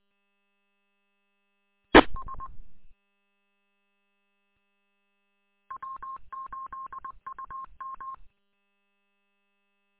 Morse code, heard as a keyed tone, about 20 wpm, 1060 Hz. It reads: ES W8UM
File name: w8um- -combined